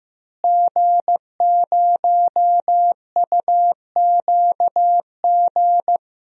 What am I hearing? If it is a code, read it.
Morse code, heard as a keyed tone, about 15 wpm, 700 Hz. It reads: G0UQG
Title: DXCC-100 Call Signs - 5 Characters - Repeated 2x - 15wpm